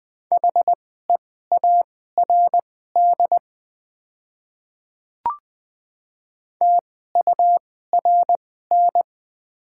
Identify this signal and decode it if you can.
Morse code, heard as a keyed tone, about 20 wpm, spaced as for 14 wpm, 700 Hz. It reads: HEARD TURN
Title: 2x Character Spacing - Top 500 Words - 20wpm